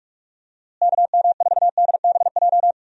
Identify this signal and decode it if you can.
Morse code, heard as a keyed tone, about 43 wpm, 700 Hz. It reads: KM4DBJ
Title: USA Call Signs - 6 Characters - Repeated 2x - 45wpm